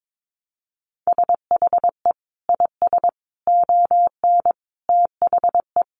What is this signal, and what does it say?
Morse code, heard as a keyed tone, about 22 wpm, 700 Hz. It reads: SHE IS ON THE
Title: Sentences from Top 1200 Words 22wpm